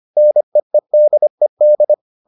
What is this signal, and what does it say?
Morse code, heard as a keyed tone, about 25 wpm, 600 Hz. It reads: NEEDED